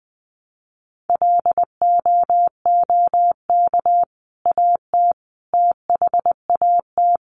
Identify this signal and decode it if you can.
Morse code, heard as a keyed tone, about 20 wpm, 700 Hz. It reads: LOOK AT THAT
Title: Sentences from Top 1100 Words 20wpm